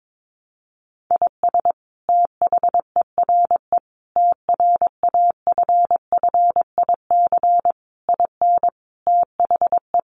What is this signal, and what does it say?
Morse code, heard as a keyed tone, about 22 wpm, 700 Hz. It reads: IS THERE TRAFFIC IN THE
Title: Sentences from Top 1300 Words 22wpm